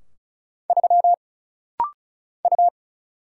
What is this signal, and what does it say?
Morse code, heard as a keyed tone, about 36 wpm, 700 Hz. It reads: 3 U
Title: Single Letter-Number Alt - Rapid-Fire 35wpm